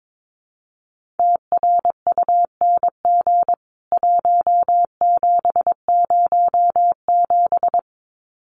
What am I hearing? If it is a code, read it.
Morse code, heard as a keyed tone, about 22 wpm, 700 Hz. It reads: TRUNG 1707
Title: CWT Exchange 22wpm